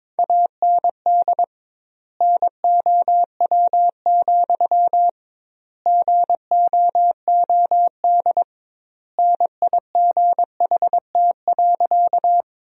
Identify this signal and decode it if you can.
Morse code, heard as a keyed tone, about 21 wpm, 700 Hz. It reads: AND NOW, GOOD NIGHT.
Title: 2x Word-Spacing - Sentences from Top 300 Words - 22wpm